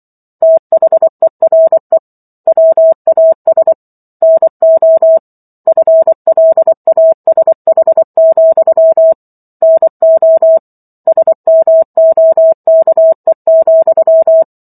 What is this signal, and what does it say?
Morse code, heard as a keyed tone, about 24 wpm, 650 Hz. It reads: THERE WAS NO FLASH, NO SMOKE,